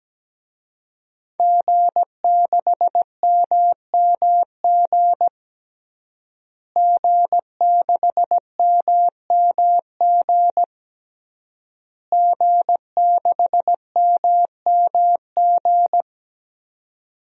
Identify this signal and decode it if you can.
Morse code, heard as a keyed tone, about 17 wpm, 700 Hz. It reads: G6MMG G6MMG G6MMG
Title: DXCC-100 Call Signs - 5 Characters - Repeated 3x - 17wpm